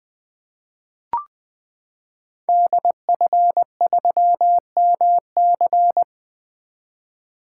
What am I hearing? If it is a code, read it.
Morse code, heard as a keyed tone, about 20 wpm, 700 Hz. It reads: DF3MC
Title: 2023 SOTA Activator Call Signs - 20wpm